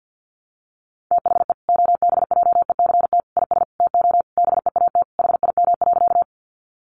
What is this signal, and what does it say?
Morse code, heard as a keyed tone, about 50 wpm, 700 Hz. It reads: THE OBJECT IS TO BEAT HIM.
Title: Sentences from Top 1300 Words 50wpm